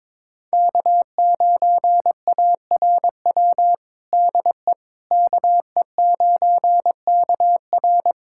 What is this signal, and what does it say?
Morse code, heard as a keyed tone, about 22 wpm, 700 Hz. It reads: K9ARW DE KE9KR
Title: USA Callsign DE Callsign - 5 Characters - 22wpm